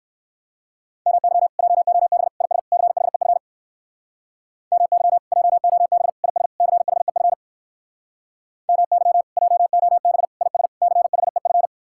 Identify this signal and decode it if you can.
Morse code, heard as a keyed tone, about 58 wpm, 700 Hz. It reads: MY JOB IS CHEF MY JOB IS CHEF MY JOB IS CHEF